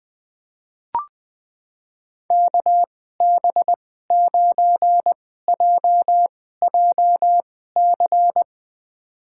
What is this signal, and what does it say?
Morse code, heard as a keyed tone, about 20 wpm, spaced as for 15 wpm, 700 Hz. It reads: KB9JJC
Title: USA Call Signs - 6 Characters - 2x Character Spacing - 20wpm